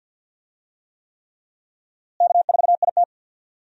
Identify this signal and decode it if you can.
Morse code, heard as a keyed tone, about 52 wpm, 700 Hz. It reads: K4IT
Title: USA Call Signs - 4 Characters - Repeated 3x - 50wpm